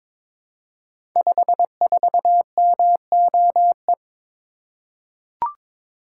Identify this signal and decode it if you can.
Morse code, heard as a keyed tone, about 22 wpm, 700 Hz. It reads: 54MOE
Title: Code Groups - 5 Letter:Numbers - 22wpm